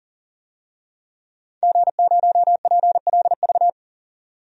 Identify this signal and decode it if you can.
Morse code, heard as a keyed tone, about 40 wpm, 700 Hz. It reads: G0PLV
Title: DXCC-100 Call Signs - 5 Characters - 40wpm